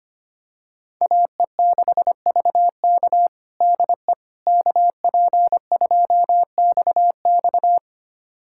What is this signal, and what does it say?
Morse code, heard as a keyed tone, about 25 wpm, 700 Hz. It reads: AE6VK DE KP2XX
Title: USA Callsign DE Callsign - 5 Characters - 25wpm